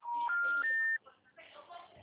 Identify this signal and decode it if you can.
Special information tone, with speech.
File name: out- -2010- - - .93123